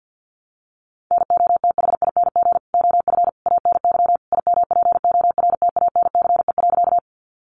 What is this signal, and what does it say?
Morse code, heard as a keyed tone, about 50 wpm, 700 Hz. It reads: NOTHING OF ANY IMPORTANCE.
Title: Sentences from Top 1200 Words 50wpm